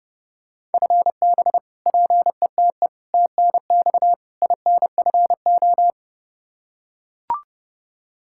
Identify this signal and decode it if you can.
Morse code, heard as a keyed tone, about 32 wpm, 700 Hz. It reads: FB PETE TNX INFO
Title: Practice QSO - Exchange 3 - More Info - 30wpm